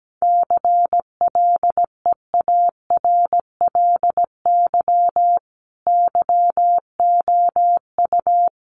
Morse code, heard as a keyed tone, about 17 wpm, 700 Hz. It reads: CLEARLY YOU
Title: Sentences from Top 1300 Words 17wpm